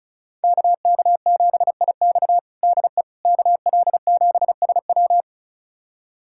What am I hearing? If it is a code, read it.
Morse code, heard as a keyed tone, about 36 wpm, 700 Hz. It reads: KK7IX DE KL7SW
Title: USA Callsign DE Callsign - 5 Characters - 35wpm